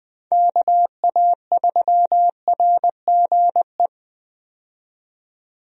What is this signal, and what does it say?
Morse code, heard as a keyed tone, about 20 wpm, 700 Hz. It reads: KA3RGE
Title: USA Call Signs - 6 Characters - 20wpm